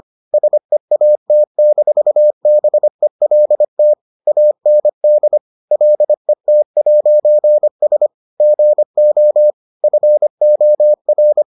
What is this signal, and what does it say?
Morse code, heard as a keyed tone, about 25 wpm, 600 Hz. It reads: SEAT-BELT AND LET'S GO FOR